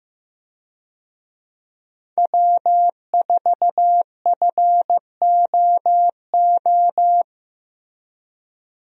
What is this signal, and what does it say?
Morse code, heard as a keyed tone, about 15 wpm, 700 Hz. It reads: W4FOO
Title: USA Call Signs - 5 Characters - Repeated 3x - 15wpm